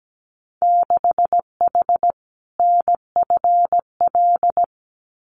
Morse code, heard as a keyed tone, about 17 wpm, 700 Hz. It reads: 6H NFL
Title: Winter Field Day Exchanges Encore 17wpm